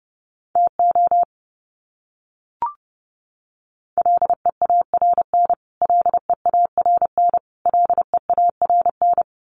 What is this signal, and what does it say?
Morse code, heard as a keyed tone, about 30 wpm, 700 Hz. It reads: TO LEARN LEARN LEARN
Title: Top 500 Words Repeated 30wpm